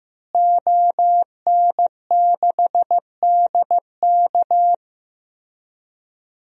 Morse code, heard as a keyed tone, about 16 wpm, 700 Hz. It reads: ON6DK
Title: DXCC-100 Call Signs - 5 Characters - Repeated 3x - 15wpm